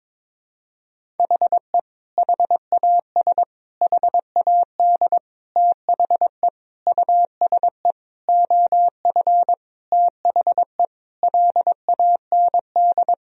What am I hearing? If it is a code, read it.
Morse code, heard as a keyed tone, about 22 wpm, 700 Hz. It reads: HE HAS HAD THE USE OF THE LAND
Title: Sentences from Top 1000 Words - Review Part 1 - 22wpm